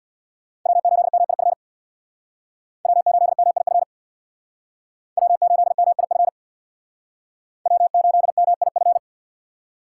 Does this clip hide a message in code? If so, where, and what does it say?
Yes: Morse code from 0.7 to 9.0 s, heard as a keyed tone, about 64 wpm, 700 Hz. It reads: W8MIF W8MIF W8MIF W8MIF